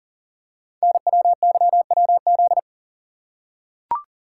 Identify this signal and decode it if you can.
Morse code, heard as a keyed tone, about 43 wpm, 700 Hz. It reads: NWYWZ
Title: Code Groups - 5 Letters - 40wpm